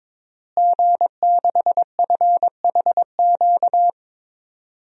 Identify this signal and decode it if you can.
Morse code, heard as a keyed tone, about 22 wpm, 700 Hz. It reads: G6FHQ
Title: DXCC-100 Call Signs - 5 Characters - Repeated 3x - 22wpm